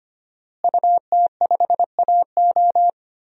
Morse code, heard as a keyed tone, about 25 wpm, 700 Hz. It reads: UT5AO
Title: DXCC-100 Call Signs - 5 Characters - Repeated 3x - 25wpm